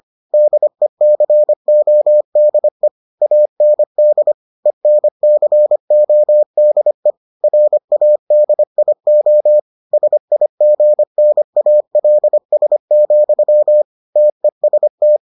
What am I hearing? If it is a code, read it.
Morse code, heard as a keyed tone, about 25 wpm, 600 Hz. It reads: DECODE AND ENCODE RADIO SIGNALS, TEST